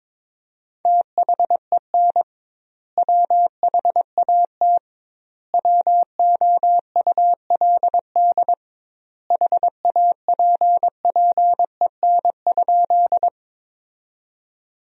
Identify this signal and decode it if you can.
Morse code, heard as a keyed tone, about 22 wpm, 700 Hz. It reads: THEN WHAT WOULD HAPPEN?
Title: 2x Word-Spacing - Sentences from Top 400 Words - 22wpm